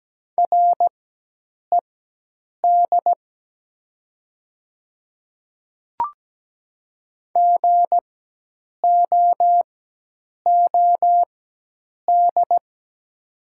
Morse code, heard as a keyed tone, about 17 wpm, spaced as for 8 wpm, 700 Hz. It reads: RED GOOD
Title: 4x Character Spacing - Top 500 Words - 17wpm